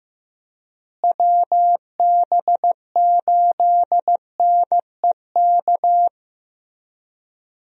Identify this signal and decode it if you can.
Morse code, heard as a keyed tone, about 16 wpm, 700 Hz. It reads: WB8NEK
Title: USA Call Signs - 6 Characters - Repeated 2x - 15wpm